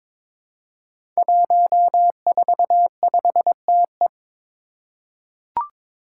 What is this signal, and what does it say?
Morse code, heard as a keyed tone, about 23 wpm, 700 Hz. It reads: 145TE
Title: Five Challenging Letter-Number Combinations 22wpm